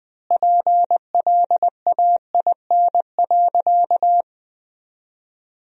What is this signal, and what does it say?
Morse code, heard as a keyed tone, about 20 wpm, 700 Hz. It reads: PLAIN.
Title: 3x Word-Spacing - Sentences from Top 400 Words - 20wpm